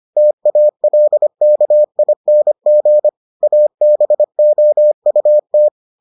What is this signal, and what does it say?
Morse code, heard as a keyed tone, about 25 wpm, 600 Hz. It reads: TALKING ABOUT